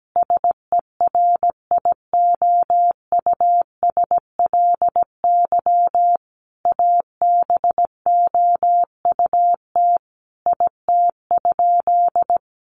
Morse code, heard as a keyed tone, about 17 wpm, 700 Hz. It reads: SERIOUSLY ABOUT IT?
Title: Sentences from Top 1200 Words 17wpm